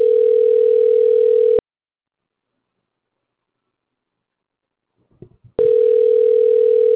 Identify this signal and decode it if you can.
Ringback tone (North America), 440 + 480 Hz.